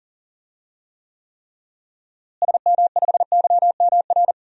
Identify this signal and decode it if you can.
Morse code, heard as a keyed tone, about 40 wpm, 700 Hz. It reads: SM5YMR
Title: DXCC-100 Call Signs - 6 Characters - Repeated 3x - 40wpm